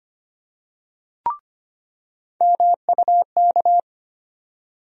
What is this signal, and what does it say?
Morse code, heard as a keyed tone, about 25 wpm, 700 Hz. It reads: MUK
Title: USA Call Sign Postfixes - 3 Letters 25wpm